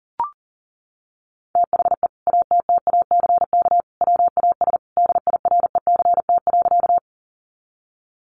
Morse code, heard as a keyed tone, about 40 wpm, 700 Hz. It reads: THE ATTACK WAS DIRECT.
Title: Sentences from Top 1500 Words 40wpm